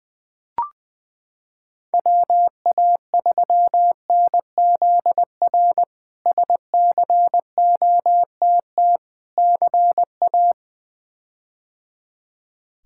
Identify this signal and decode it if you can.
Morse code, heard as a keyed tone, about 20 wpm, 700 Hz. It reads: WA3NZR SCOTT CA
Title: K1USN SST QSO Practice - 20wpm